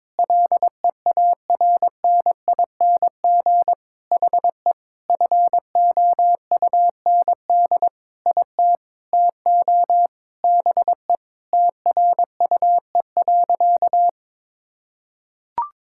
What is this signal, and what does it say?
Morse code, heard as a keyed tone, about 22 wpm, 700 Hz. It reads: LEARNING HE FOUND IT TO BE TRUE.
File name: practice / 22wpm / Sentences with 4-Word Phrases - 22wpm